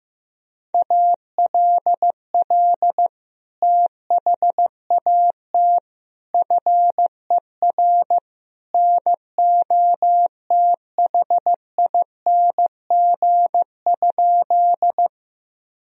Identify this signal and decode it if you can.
Morse code, heard as a keyed tone, about 15 wpm, 700 Hz. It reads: ALL THAT FER NOTHING?
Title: Sentences from Top 500 Words and Abbreviations 15wpm